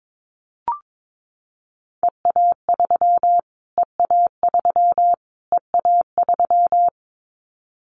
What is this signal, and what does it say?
Morse code, heard as a keyed tone, about 22 wpm, 700 Hz. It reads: EA3 EA3 EA3
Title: Callsign Prefixes for Top 25 DXCC - Repeated 3x - 22wpm